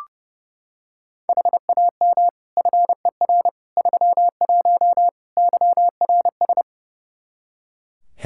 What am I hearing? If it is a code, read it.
Morse code, heard as a keyed tone, about 30 wpm, 700 Hz. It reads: HAM FER 31 YRS
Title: Practice QSO - Exchange 3 - More Info - 30wpm